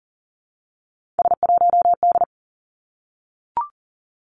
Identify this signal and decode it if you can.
Morse code, heard as a keyed tone, about 42 wpm, 700 Hz. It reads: S1D
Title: Three Challenging Letter-Number Combinations 40wpm